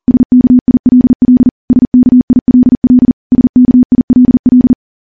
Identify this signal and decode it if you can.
Morse code, heard as a keyed tone, about 40 wpm, 250 Hz. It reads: SKILL SKILL SKILL
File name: morse / morse-training-40WPM